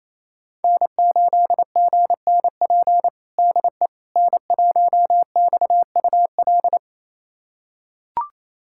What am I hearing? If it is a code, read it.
Morse code, heard as a keyed tone, about 27 wpm, 690 Hz. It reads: N8GNP DE N1XUL
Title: USA Callsign DE Callsign - 5 Characters - 28wpm